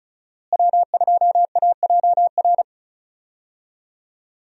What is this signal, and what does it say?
Morse code, heard as a keyed tone, about 36 wpm, 700 Hz. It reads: W2AJR
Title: USA Call Signs - 5 Characters - Repeated 3x - 35wpm